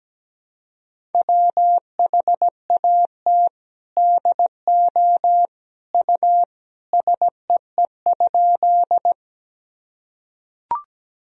Morse code, heard as a keyed tone, about 17 wpm, 700 Hz. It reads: WHAT DO U SEE?
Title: Sentences from Top 100 Words and Abbreviations 17wpm